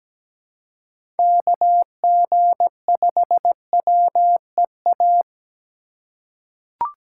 Morse code, heard as a keyed tone, about 17 wpm, 700 Hz. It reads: KG5WEA
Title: USA Call Signs - 6 Characters - 4x Character Spacing - 17wpm